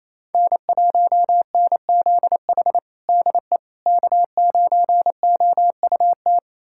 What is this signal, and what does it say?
Morse code, heard as a keyed tone, about 28 wpm, 700 Hz. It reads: N1NZH DE K9OUT
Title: USA Callsign DE Callsign - 5 Characters - 28wpm